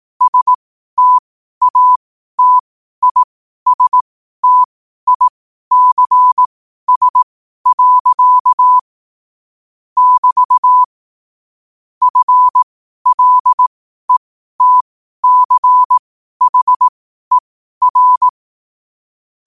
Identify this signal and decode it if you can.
Morse code, heard as a keyed tone, about 18 wpm, spaced as for 12 wpm, 1000 Hz. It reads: STATISTICS. <BT> FLETCHER
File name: qotdfiles / 12wpm